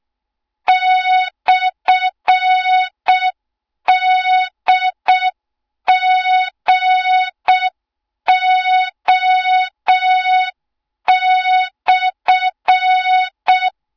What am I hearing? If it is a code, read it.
Morse code, heard as a keyed tone, about 6 wpm, 3700 Hz. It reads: /DGO/
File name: Morze-HA2MN / hang / 20 GYAKORLAT